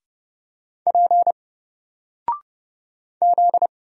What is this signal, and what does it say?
Morse code, heard as a keyed tone, about 30 wpm, 700 Hz. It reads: P Z
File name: Single-Letter-Number-30wpm